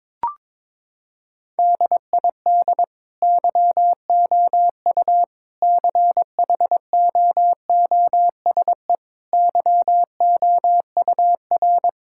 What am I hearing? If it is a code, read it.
Morse code, heard as a keyed tone, about 22 wpm, 700 Hz. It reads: DID YOU CHOOSE YOUR
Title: Sentences from Top 1000 Words - Review Part 6 - 22wpm